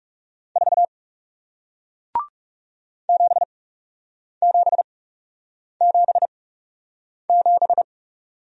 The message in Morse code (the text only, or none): 4 7777